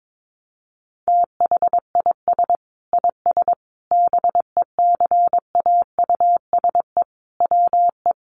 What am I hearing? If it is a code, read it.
Morse code, heard as a keyed tone, about 22 wpm, 700 Hz. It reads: THIS IS BECAUSE WE
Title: Sentences from Top 1200 Words 22wpm